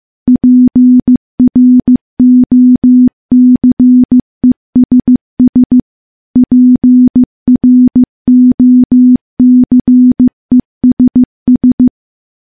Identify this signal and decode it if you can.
Morse code, heard as a keyed tone, about 15 wpm, 250 Hz. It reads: PROCESS PROCESS
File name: morse / morse-training-15WPM